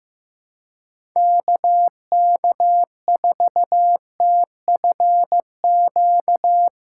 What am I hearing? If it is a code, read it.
Morse code, heard as a keyed tone, about 16 wpm, 700 Hz. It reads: KK4TFQ
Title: USA Callsign DE Callsign - 6 Characters - 15wpm